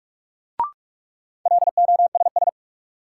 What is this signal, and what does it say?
Morse code, heard as a keyed tone, about 45 wpm, 700 Hz. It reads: LOSS
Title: Sentences from Top 1400 Words - New Word Drill Alt - 45wpm